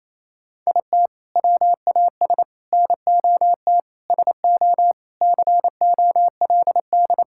Morse code, heard as a keyed tone, about 28 wpm, 700 Hz. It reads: IT WAS NOT SO COLD
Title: Sentences from Top 1600 Words 28wpm